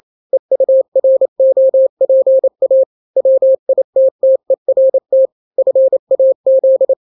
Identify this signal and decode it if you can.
Morse code, heard as a keyed tone, about 28 wpm, 530 Hz. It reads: EUROPA WITTERT FAZ
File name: daily / 27wpm